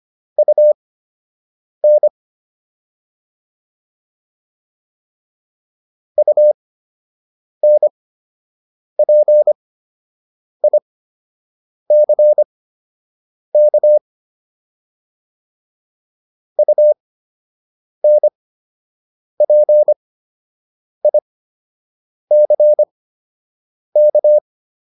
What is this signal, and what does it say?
Morse code, heard as a keyed tone, about 25 wpm, spaced as for 7 wpm, 600 Hz. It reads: UN UNPICK UNPICK